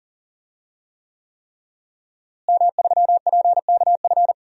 Morse code, heard as a keyed tone, about 40 wpm, 700 Hz. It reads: M3PKF
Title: DXCC-100 Call Signs - 5 Characters - Repeated 3x - 40wpm